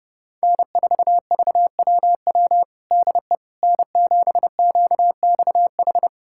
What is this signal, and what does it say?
Morse code, heard as a keyed tone, about 30 wpm, 700 Hz. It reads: N4VWW DE N7QXH
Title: USA Callsign DE Callsign - 5 Characters - 30wpm